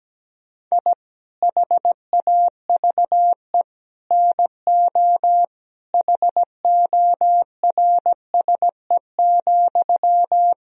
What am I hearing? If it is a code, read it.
Morse code, heard as a keyed tone, about 17 wpm, 700 Hz. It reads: I HAVE NO HORSE,
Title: Sentences from Top 1000 Words - Review Part 1 - 17wpm